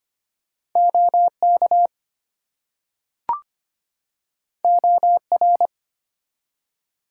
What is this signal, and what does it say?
Morse code, heard as a keyed tone, about 25 wpm, 700 Hz. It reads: OK OR
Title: Two Letter Words - 25wpm